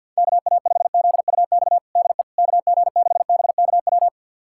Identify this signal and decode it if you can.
Morse code, heard as a keyed tone, about 46 wpm, 700 Hz. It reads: KA5ZUX DE KG6BKW
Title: USA Callsign DE Callsign 50wpm